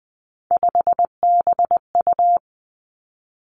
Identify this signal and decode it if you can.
Morse code, heard as a keyed tone, about 20 wpm, 700 Hz. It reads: 5BU
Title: Three Challenging Letter-Number Combinations 20wpm